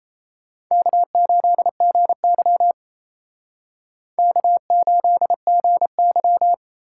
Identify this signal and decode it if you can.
Morse code, heard as a keyed tone, about 35 wpm, 700 Hz. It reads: K8GY K8GY
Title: Speed Racing - USA 1x2 Call Signs 28wpm